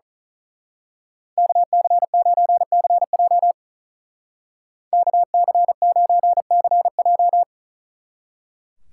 Morse code, heard as a keyed tone, about 39 wpm, 700 Hz. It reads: KC9CJ KC9CJ
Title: Speed Racing - USA 2x2 Call Signs 35wpm